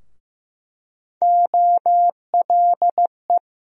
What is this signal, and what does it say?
Morse code, heard as a keyed tone, about 16 wpm, 700 Hz. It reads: OLE
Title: USA Call Sign Postfixes - 3 Letters 15wpm